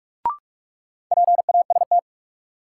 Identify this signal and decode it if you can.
Morse code, heard as a keyed tone, about 42 wpm, 700 Hz. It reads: PAST
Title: Sets of 2 Words - New Word Drill - 45wpm